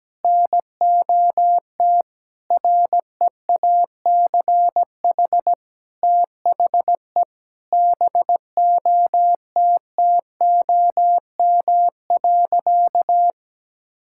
Morse code, heard as a keyed tone, about 17 wpm, 700 Hz. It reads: NOT REACH THE BOTTOM.
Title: Sentences from Top 1000 Words - Review Part 6 - 17wpm